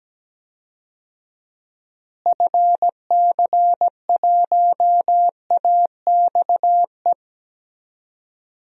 Morse code, heard as a keyed tone, about 17 wpm, 700 Hz. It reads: FC1AXE
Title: DXCC-100 Call Signs - 6 Characters - Repeated 3x - 17wpm